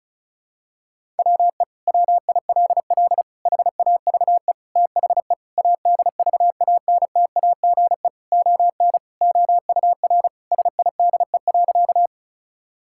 Morse code, heard as a keyed tone, about 34 wpm, 700 Hz. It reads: WE WILL HAVE THE ADVANTAGE ON OUR SIDE.
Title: Sentences from Top 1000 Words - Review Part 6 - 35wpm